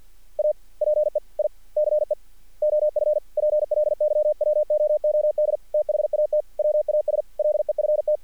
Morse code, heard as a keyed tone, about 50 wpm, 600 Hz. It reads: A PEICE OF PLYWOOD THAT WAS LEFT
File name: archived / CRSnet